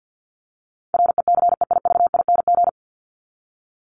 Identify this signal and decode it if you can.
Morse code, heard as a keyed tone, about 50 wpm, 700 Hz. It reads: RECEIVING
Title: ING Suffix - Top 100 Words with Base Words 50wpm